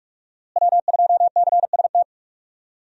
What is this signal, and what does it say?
Morse code, heard as a keyed tone, about 41 wpm, 700 Hz. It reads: W2CST
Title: USA Call Signs - 5 Characters - Repeated 2x - 45wpm